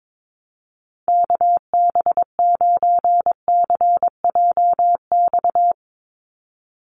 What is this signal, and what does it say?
Morse code, heard as a keyed tone, about 22 wpm, 700 Hz. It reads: KB9CJX
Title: USA Call Signs 22wpm